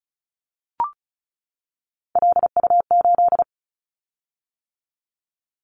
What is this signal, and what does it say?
Morse code, heard as a keyed tone, about 34 wpm, 700 Hz. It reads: LU8
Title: Callsign Prefixes for Top 50 DXCC - 35wpm